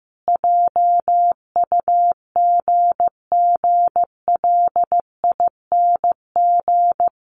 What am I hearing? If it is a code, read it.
Morse code, heard as a keyed tone, about 15 wpm, 700 Hz. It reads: JUGGLING
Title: Easy Phrases 15wpm